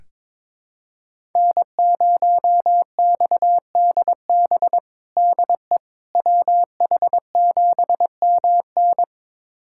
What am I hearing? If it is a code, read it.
Morse code, heard as a keyed tone, about 22 wpm, 700 Hz. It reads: N0XDB DE WH7MN
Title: USA Callsign DE Callsign - 5 Characters - 22wpm